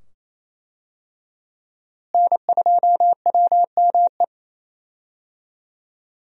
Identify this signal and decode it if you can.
Morse code, heard as a keyed tone, about 28 wpm, 700 Hz. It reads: N2WME
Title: USA Call Signs - 5 Characters - Repeated 2x - 28wpm